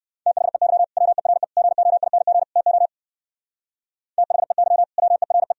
Morse code, heard as a keyed tone, about 58 wpm, 700 Hz. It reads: THEY WERE GOING TO THEY WERE